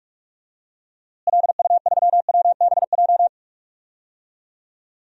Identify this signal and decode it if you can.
Morse code, heard as a keyed tone, about 41 wpm, 700 Hz. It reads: LU3WBJ
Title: DXCC-100 Call Signs - 6 Characters - Repeated 3x - 45wpm